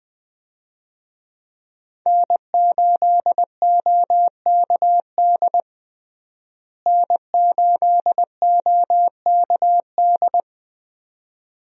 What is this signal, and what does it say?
Morse code, heard as a keyed tone, about 20 wpm, 700 Hz. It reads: N8OKD N8OKD
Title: USA Call Signs - 5 Characters - Repeated 2x - 20wpm